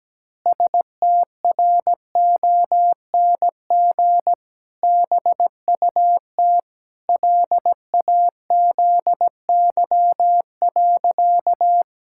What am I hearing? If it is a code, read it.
Morse code, heard as a keyed tone, about 17 wpm, 700 Hz. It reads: STRONG BUT LAZY.
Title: Pangram Challenge v2 - 17wpm